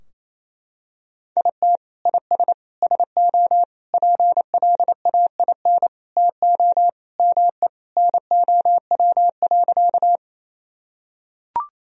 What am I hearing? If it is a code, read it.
Morse code, heard as a keyed tone, about 28 wpm, 700 Hz. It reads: IT IS SO PLAIN TO ME NOW.
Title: Sentences from Top 1000 Words - Review Part 2 - 28wpm